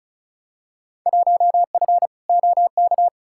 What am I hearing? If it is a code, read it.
Morse code, heard as a keyed tone, about 36 wpm, 690 Hz. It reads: 1F OK
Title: Field Day Exchanges Encore - 35wpm